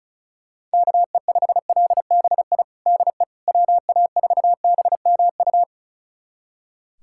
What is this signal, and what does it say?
Morse code, heard as a keyed tone, about 36 wpm, 700 Hz. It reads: KE5LBI DE WA4BMU
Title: USA Callsign DE Callsign - 6 Characters - 35wpm